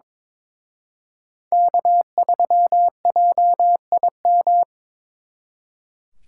Morse code, heard as a keyed tone, about 25 wpm, 700 Hz. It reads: K3JIM
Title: Speed Racing - USA 1x3 Call Signs 22wpm